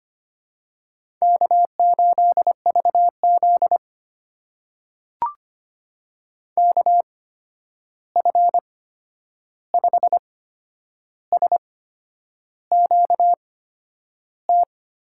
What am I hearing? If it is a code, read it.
Morse code, heard as a keyed tone, about 25 wpm, spaced as for 9 wpm, 700 Hz. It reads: K8VZ KF5SQT
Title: USA Call Signs - 8x Character Spacing - 25wpm